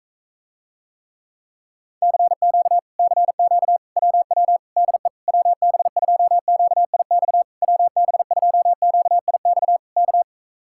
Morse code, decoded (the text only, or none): CQ CQ WW DE WB2QIX WB2QIX K